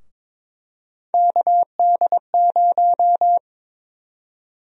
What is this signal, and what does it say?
Morse code, heard as a keyed tone, about 22 wpm, 700 Hz. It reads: KD0
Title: USA Call Sign Prefixes - 22wpm